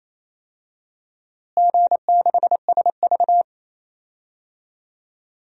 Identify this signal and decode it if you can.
Morse code, heard as a keyed tone, about 28 wpm, 700 Hz. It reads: G6SV
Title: DXCC-100 Call Signs - 4 Characters - Repeated 2x - 28wpm